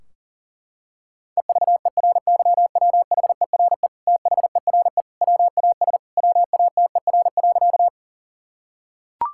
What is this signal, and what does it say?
Morse code, heard as a keyed tone, about 40 wpm, 700 Hz. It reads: EVERYWHERE THERE WAS WATER.
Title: Sentences from Top 1600 Words 40wpm